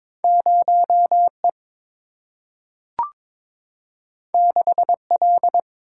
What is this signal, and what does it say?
Morse code, heard as a keyed tone, about 22 wpm, 700 Hz. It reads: 0 E 6 L
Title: Single-Letter Single-Number 22wpm